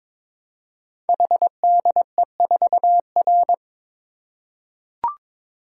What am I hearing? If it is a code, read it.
Morse code, heard as a keyed tone, about 22 wpm, 700 Hz. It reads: HDE4R
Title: Code Groups - 5 Letter:Numbers - 22wpm